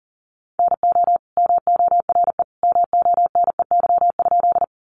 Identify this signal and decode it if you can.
Morse code, heard as a keyed tone, about 40 wpm, 700 Hz. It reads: NO MORE MONEY?
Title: Sentences from Top 400 Words 40wpm